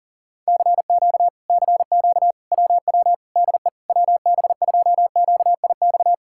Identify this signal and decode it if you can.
Morse code, heard as a keyed tone, about 40 wpm, 700 Hz. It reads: CQ CQ WW DE WB2QIX
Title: Speed-Racing Common QSO Phrases - 40wpm